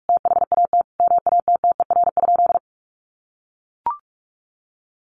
Morse code, heard as a keyed tone, about 44 wpm, 700 Hz. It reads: THAT MATTER?
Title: Sentences from Top 700 Words 45wpm